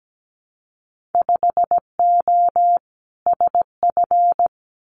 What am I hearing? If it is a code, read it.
Morse code, heard as a keyed tone, about 17 wpm, 700 Hz. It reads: 5O SF
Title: Winter Field Day Exchanges Encore 17wpm